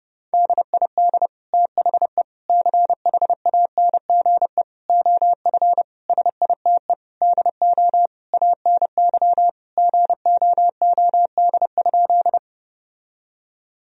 Morse code, heard as a keyed tone, about 30 wpm, 700 Hz. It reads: DID THE CHANGE OF SITE DO ANY GOOD?
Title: Sentences from Top 1000 Words - Review Part 4 - 30wpm